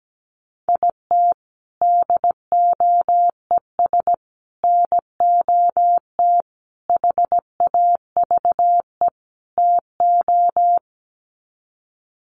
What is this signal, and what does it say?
Morse code, heard as a keyed tone, about 17 wpm, 700 Hz. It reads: IT DOES NOT HAVE TO
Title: Sets of 5 Words 17wpm